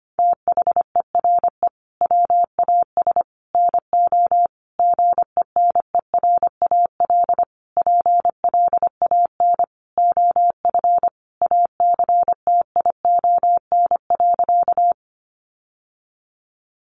Morse code, heard as a keyed tone, about 25 wpm, 700 Hz. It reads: THERE WAS NO GENERAL PLAN OF ACTION.
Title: Sentences from Top 700 Words 25wpm